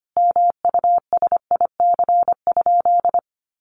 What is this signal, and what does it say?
Morse code, heard as a keyed tone, about 25 wpm, 700 Hz. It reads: MUSIC?
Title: Sentences from Top 300 Words 25wpm